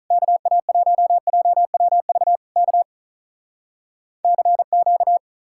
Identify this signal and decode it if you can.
Morse code, heard as a keyed tone, about 46 wpm, 700 Hz. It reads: KA1JWV K CQ